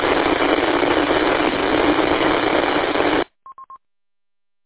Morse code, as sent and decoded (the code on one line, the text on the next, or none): none